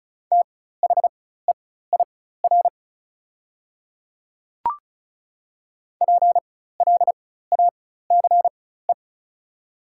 Morse code, heard as a keyed tone, about 36 wpm, spaced as for 16 wpm, 700 Hz. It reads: THEIR PLACE
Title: 4x Character Spacing - Top 200 Words - 35wpm